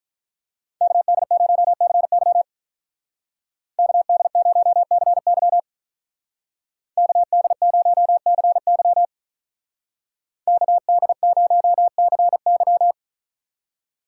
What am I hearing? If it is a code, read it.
Morse code, heard as a keyed tone, about 43 wpm, 700 Hz. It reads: KD0CY KD0CY KD0CY KD0CY
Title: Speed Racing - USA 2x2 Call Signs 35wpm